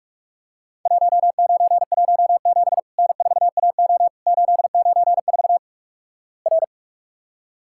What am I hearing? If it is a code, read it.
Morse code, heard as a keyed tone, about 46 wpm, 700 Hz. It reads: 1917 N4AO 894 R
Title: Worked All Europe DX Contest - QTC Exchange - 45wpm